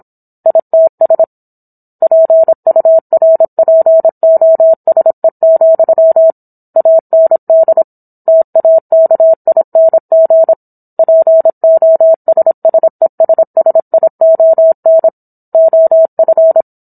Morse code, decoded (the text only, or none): ITS PURPOSE, AND TAKING POSSESSION OF